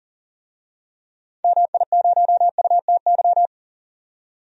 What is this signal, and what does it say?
Morse code, heard as a keyed tone, about 40 wpm, 700 Hz. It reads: MI0UTY
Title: DXCC-100 Call Signs - 6 Characters - 40wpm